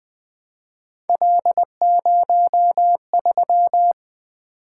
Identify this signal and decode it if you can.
Morse code, heard as a keyed tone, about 23 wpm, 700 Hz. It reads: L03
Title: Speed-Racing - Three Letter-Number Combinations - 20wpm